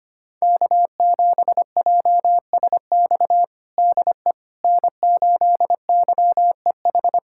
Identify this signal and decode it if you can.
Morse code, heard as a keyed tone, about 25 wpm, 700 Hz. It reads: K7JSX DE N8YEH
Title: USA Callsign DE Callsign - 5 Characters - 25wpm